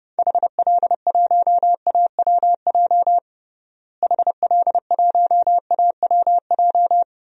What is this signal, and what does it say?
Morse code, heard as a keyed tone, about 30 wpm, 700 Hz. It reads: HL1AWJ HL1AWJ
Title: DXCC-100 Call Signs - 6 Characters - Repeated 2x - 30wpm